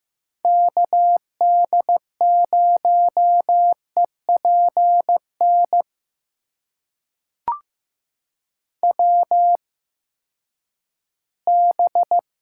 Morse code, heard as a keyed tone, about 15 wpm, spaced as for 10 wpm, 700 Hz. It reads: KD0EPN WB
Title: USA Call Signs - 8x Character Spacing - 15wpm